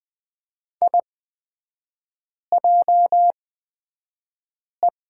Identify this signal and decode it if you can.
Morse code, heard as a keyed tone, about 20 wpm, spaced as for 5 wpm, 700 Hz. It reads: IJE
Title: 224 Lesson 36e - BT - Three Characters 20wpm